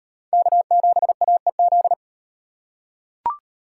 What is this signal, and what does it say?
Morse code, heard as a keyed tone, about 38 wpm, 700 Hz. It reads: K7AEZ